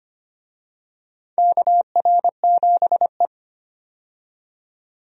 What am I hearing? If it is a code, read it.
Morse code, heard as a keyed tone, about 25 wpm, 700 Hz. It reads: KR7E
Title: USA Call Signs - 4 Characters - 25wpm